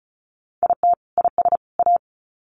Morse code, heard as a keyed tone, about 34 wpm, 700 Hz. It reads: IT IS A
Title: Sets of 3 Words 35wpm